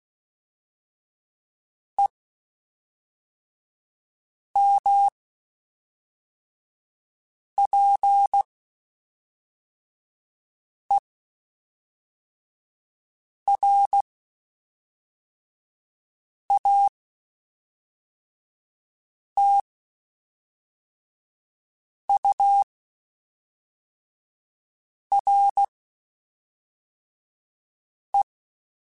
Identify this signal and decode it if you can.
Morse code, heard as a keyed tone, about 16 wpm, spaced as for 3 wpm, 800 Hz. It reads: EMPERATURE
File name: Spaced Characters / Spaced Characters-mp3s / Weather 3 17 sp @5